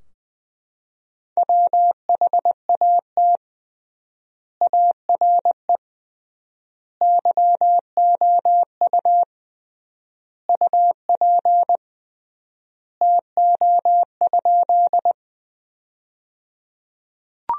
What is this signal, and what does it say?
Morse code, heard as a keyed tone, about 21 wpm, 700 Hz. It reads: WHAT ARE YOU UP TO?
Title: 3x Word-Spacing - Sentences from Top 100 Words - 20wpm